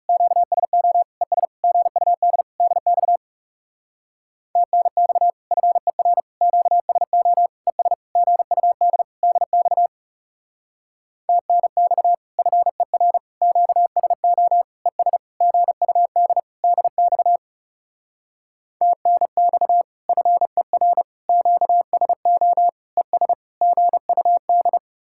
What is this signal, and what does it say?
Morse code, heard as a keyed tone, about 38 wpm, 700 Hz. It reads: QSO ES GUD DX TNX FER QSO ES GUD DX TNX FER QSO ES GUD DX TNX FER QSO ES GUD